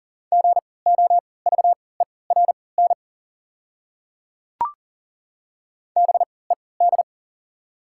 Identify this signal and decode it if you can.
Morse code, heard as a keyed tone, about 40 wpm, spaced as for 23 wpm, 690 Hz. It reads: GOVERN BED
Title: 3x Character Spacing - Top 500 Words - 40wpm